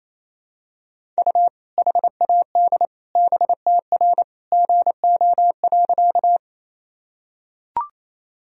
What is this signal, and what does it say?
Morse code, heard as a keyed tone, about 28 wpm, 700 Hz. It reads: U HAD BTR GO.
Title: Sentences from Top 400 Words and Abbreviations 28wpm